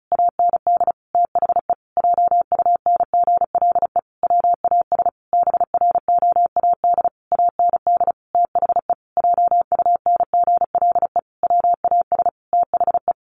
Morse code, heard as a keyed tone, about 34 wpm, 700 Hz. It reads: AND THE JUNGLE WAS BROAD AND THE JUNGLE WAS THE
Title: A Sound Of Thunder 35wpm